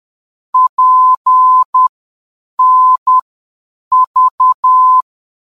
Morse code, heard as a keyed tone, about 10 wpm, spaced as for 7 wpm, 1020 Hz. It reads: PNV
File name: PNV-VOR DME